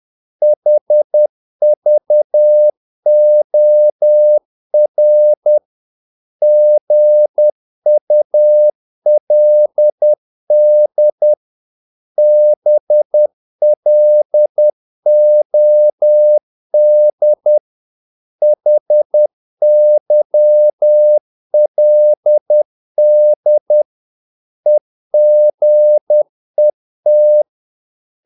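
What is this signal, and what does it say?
Morse code, heard as a keyed tone, about 10 wpm, 600 Hz. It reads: HVOR GULD BLOD HYLD EGET
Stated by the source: Morse practice of short Danish words at 10 wpm, from CW med Gnister.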